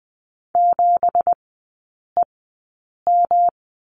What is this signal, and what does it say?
Morse code, heard as a keyed tone, about 20 wpm, spaced as for 8 wpm, 700 Hz. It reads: 7EM
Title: 166 Lesson 27e - Number 7 - Three Characters 20wpm